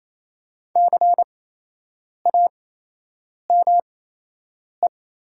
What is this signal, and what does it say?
Morse code, heard as a keyed tone, about 28 wpm, spaced as for 8 wpm, 700 Hz. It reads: CAME
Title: 8x Character Spacing - Top 300 Words - 28wpm